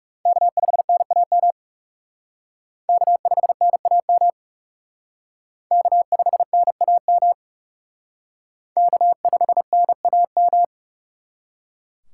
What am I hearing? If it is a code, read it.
Morse code, heard as a keyed tone, about 37 wpm, 700 Hz. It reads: K5NAM K5NAM K5NAM K5NAM
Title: Speed Racing - USA 1x3 Call Signs 30wpm